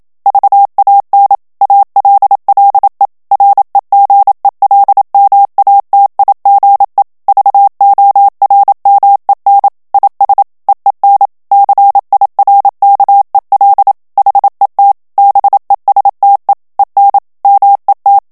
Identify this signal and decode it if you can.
Morse code, heard as a keyed tone, about 27 wpm, 800 Hz. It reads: VAN ALLE REGELMATIGE VORMEN IS EEN CIRKEL HET BESTE EN MET
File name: HOLLANDAIS texte clair 1 vitesse 26 mots minute